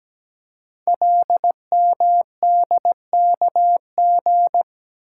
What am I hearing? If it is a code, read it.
Morse code, heard as a keyed tone, about 17 wpm, 700 Hz. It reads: LMDKG
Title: Code Groups - 5 Letters - 17wpm